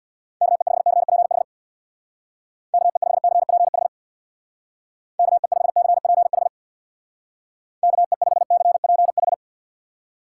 Morse code, heard as a keyed tone, about 64 wpm, 700 Hz. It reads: KE5CPH KE5CPH KE5CPH KE5CPH